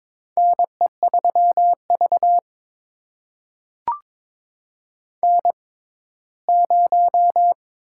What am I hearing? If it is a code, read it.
Morse code, heard as a keyed tone, about 22 wpm, spaced as for 15 wpm, 700 Hz. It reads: NE3V N0
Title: USA Call Signs - 4 Characters - 6x Character Spacing - 22wpm